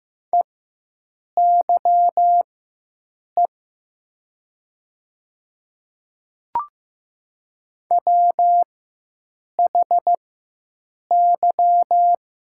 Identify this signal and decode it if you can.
Morse code, heard as a keyed tone, about 15 wpm, spaced as for 7 wpm, 700 Hz. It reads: EYE WHY
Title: 4x Character Spacing - Top 300 Words - 15wpm